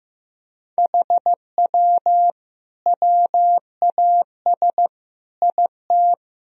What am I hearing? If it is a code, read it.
Morse code, heard as a keyed tone, about 14 wpm, 700 Hz. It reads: HW WAS IT
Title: Sentences from Top 500 Words and Abbreviations 15wpm